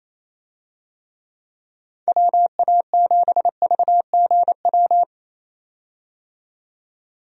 Morse code, heard as a keyed tone, about 27 wpm, 700 Hz. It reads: WA7VGW
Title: USA Call Signs - 6 Characters - 28wpm